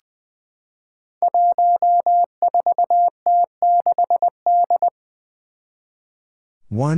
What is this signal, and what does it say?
Morse code, heard as a keyed tone, about 20 wpm, 700 Hz. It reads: 14T6D
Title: Five Challenging Letter-Number Combinations 20wpm